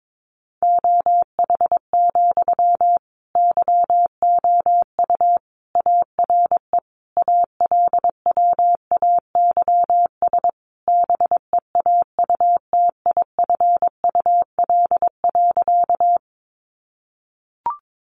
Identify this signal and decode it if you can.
Morse code, heard as a keyed tone, about 22 wpm, 700 Hz. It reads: OH, YOU ARE ALWAYS BEAUTIFUL.
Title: Sentences from Top 900 Words 22wpm